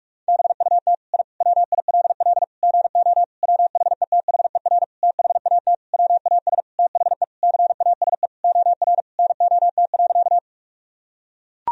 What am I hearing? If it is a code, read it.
Morse code, heard as a keyed tone, about 41 wpm, 700 Hz. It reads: BUT I WILL GO WHETHER THAT WAS THE CASE OR NOT.
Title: Sentences from Top 1000 Words - Review Part 3 - 45wpm